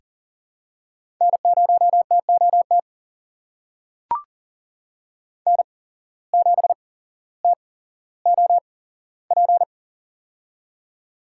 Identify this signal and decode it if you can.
Morse code, heard as a keyed tone, about 41 wpm, spaced as for 17 wpm, 700 Hz. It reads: N0TOT N7TOP
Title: USA Call Signs - 5 Characters - 8x Character Spacing - 40wpm